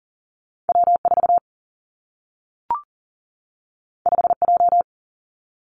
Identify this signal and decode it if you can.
Morse code, heard as a keyed tone, about 41 wpm, 700 Hz. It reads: W4 5J
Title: Two Challenging Letter-Number Combinations 40wpm